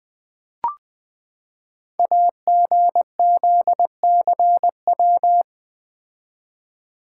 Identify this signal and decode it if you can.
Morse code, heard as a keyed tone, about 20 wpm, 700 Hz. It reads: AGZCW
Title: Code Groups - 5 Letters - 20wpm